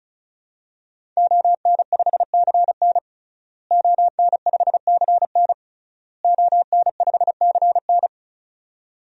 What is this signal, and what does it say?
Morse code, heard as a keyed tone, about 36 wpm, 700 Hz. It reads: ON5CN ON5CN ON5CN
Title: DXCC-100 Call Signs - 5 Characters - Repeated 3x - 35wpm